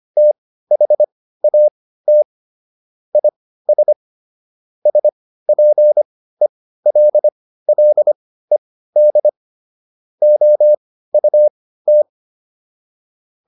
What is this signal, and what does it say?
Morse code, heard as a keyed tone, about 25 wpm, spaced as for 15 wpm, 600 Hz. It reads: THAT IS SPELLED OUT